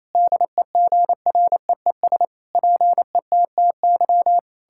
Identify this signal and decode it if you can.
Morse code, heard as a keyed tone, about 28 wpm, 700 Hz. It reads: DEGREES PETTY
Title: Pangram Challenge v2 - 28wpm